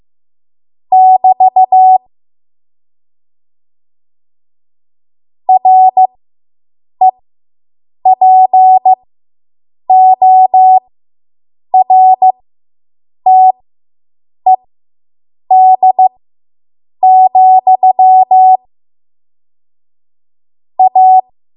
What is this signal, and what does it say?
Morse code, heard as a keyed tone, about 15 wpm, spaced as for 7 wpm, 750 Hz. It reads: <BT> REPORTED, A